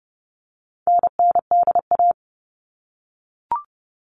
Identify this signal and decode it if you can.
Morse code, heard as a keyed tone, about 30 wpm, 700 Hz. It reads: NNDA
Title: All Cut Numbers - 4-Digit 30wpm